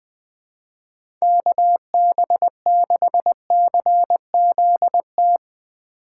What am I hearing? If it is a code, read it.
Morse code, heard as a keyed tone, about 20 wpm, 700 Hz. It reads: KB6CZT
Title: USA Call Signs - 6 Characters - 3x Character Spacing - 20wpm